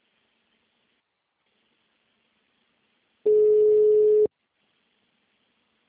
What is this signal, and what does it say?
Ringback tone (Europe, 425 Hz).
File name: out- -2003- - - .35